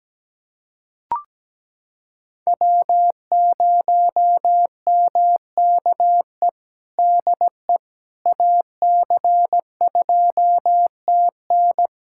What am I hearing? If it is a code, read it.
Morse code, heard as a keyed tone, about 17 wpm, 700 Hz. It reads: W0MKE DE AC2TN
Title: USA Callsign DE Callsign - 5 Characters - 17wpm